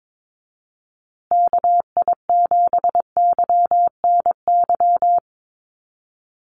Morse code, heard as a keyed tone, about 22 wpm, 700 Hz. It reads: KI7YNY
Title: USA Call Signs Repeated 3x 22wpm